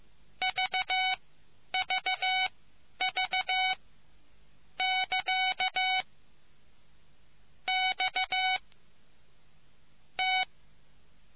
Morse code, heard as a keyed tone, about 15 wpm, spaced as for 8 wpm, 2250 Hz. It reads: VVV<KA> X T